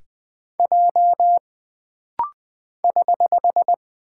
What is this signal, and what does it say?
Morse code, heard as a keyed tone, about 20 wpm, 700 Hz. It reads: J <HH>
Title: Single Character (Letter:Number:Punctuation:Prosign) - Rapid-Fire - 20wpm